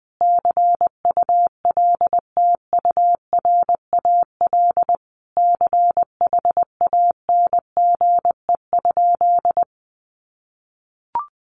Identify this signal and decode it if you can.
Morse code, heard as a keyed tone, about 20 wpm, 700 Hz. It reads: CULTURAL CHANGE?
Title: Sentences from Top 1500 Words 20wpm